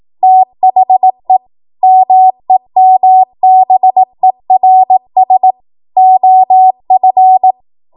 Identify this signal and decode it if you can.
Morse code, heard as a keyed tone, about 18 wpm, 750 Hz. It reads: THE MEMBERS OF
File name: Morse / 18WPM